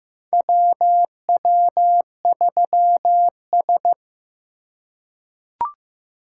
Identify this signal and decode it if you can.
Morse code, heard as a keyed tone, about 15 wpm, 700 Hz. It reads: WW3S
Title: USA Call Signs - 4 Characters - 4x Character Spacing - 15wpm